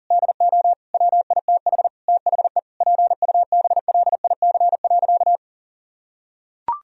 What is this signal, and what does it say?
Morse code, heard as a keyed tone, about 37 wpm, 700 Hz. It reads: DO WITH THE PUBLIC.
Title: Sentences from Top 1000 Words - Review Part 3 - 40wpm